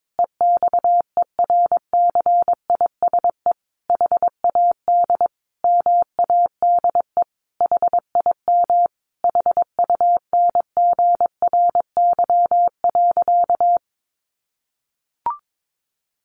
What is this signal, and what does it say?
Morse code, heard as a keyed tone, about 22 wpm, 700 Hz. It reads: EXERCISE HAD MADE HIM HUNGRY.
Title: Sentences from Top 1400 Words 22wpm